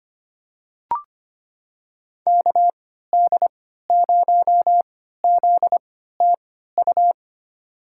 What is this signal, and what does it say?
Morse code, heard as a keyed tone, about 25 wpm, spaced as for 14 wpm, 700 Hz. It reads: KD0ZTU
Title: USA Call Signs - 6 Characters - 3x Character Spacing - 25wpm